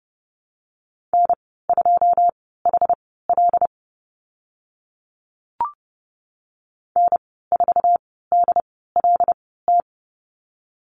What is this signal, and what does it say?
Morse code, heard as a keyed tone, about 30 wpm, spaced as for 17 wpm, 700 Hz. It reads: N2HL N4DLT
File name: Lesson 15g - Number 5 - Call Signs 30wpm